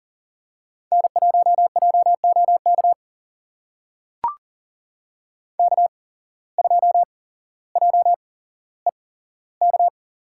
Morse code, heard as a keyed tone, about 38 wpm, spaced as for 17 wpm, 700 Hz. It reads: N1JOK K2JEK
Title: USA Call Signs - 5 Characters - 8x Character Spacing - 40wpm